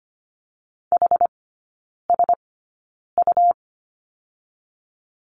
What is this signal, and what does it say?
Morse code, heard as a keyed tone, about 25 wpm, spaced as for 9 wpm, 700 Hz. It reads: HSU
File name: Lesson 17e - Letter u - Three Characters 25wpm